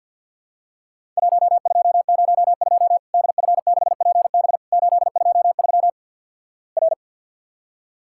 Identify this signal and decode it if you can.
Morse code, heard as a keyed tone, about 50 wpm, 700 Hz. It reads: 1202 DF6PB 823 R
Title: Worked All Europe DX Contest - QTC Exchange - 50wpm